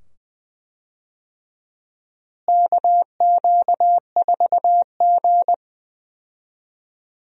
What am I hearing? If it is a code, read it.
Morse code, heard as a keyed tone, about 20 wpm, 700 Hz. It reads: KQ4G
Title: USA Call Signs - 4 Characters - Repeated 2x - 20wpm